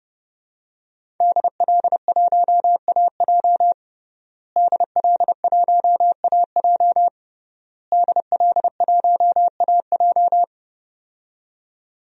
Morse code, heard as a keyed tone, about 32 wpm, 700 Hz. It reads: DL1AJ DL1AJ DL1AJ
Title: DXCC-100 Call Signs - 5 Characters - Repeated 3x - 30wpm